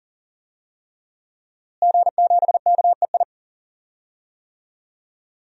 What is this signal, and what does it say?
Morse code, heard as a keyed tone, about 40 wpm, 700 Hz. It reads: G7KEI
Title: DXCC-100 Call Signs - 5 Characters - Repeated 3x - 40wpm